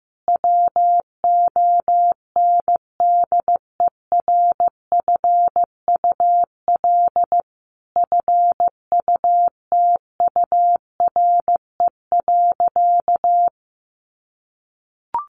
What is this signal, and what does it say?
Morse code, heard as a keyed tone, about 15 wpm, 700 Hz. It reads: WONDERFUL FUTURE.
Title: Sentences from Top 1100 Words 15wpm